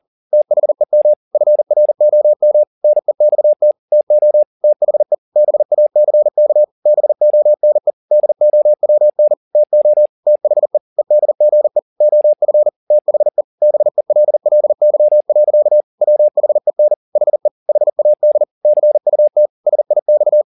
Morse code, heard as a keyed tone, about 40 wpm, 600 Hz. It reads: THEM FROM NEXT TO THE BACK BONE DOWN TO THE EDGE OF THE BELLY. WHEN HE HAD CUT SIX